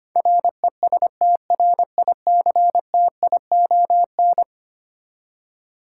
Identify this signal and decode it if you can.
Morse code, heard as a keyed tone, about 25 wpm, 700 Hz. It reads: RESTRICTION
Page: morse-code-ninja-practice-25wpm